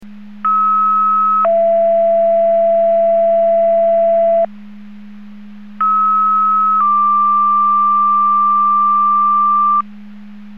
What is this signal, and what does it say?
Two-tone page, heard as a one-second tone, then a three-second tone, about 1290 then 670 Hz, about 1290 then 1180 Hz. It is heard from 0.4 to 9.8 s.
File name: Portage / 00270 05-45 Portage 154.385 Paging